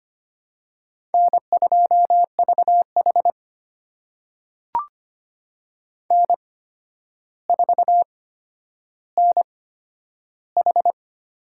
Morse code, heard as a keyed tone, about 25 wpm, spaced as for 11 wpm, 700 Hz. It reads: N2VH N4NH
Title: USA Call Signs - 4 Characters - 8x Character Spacing - 25wpm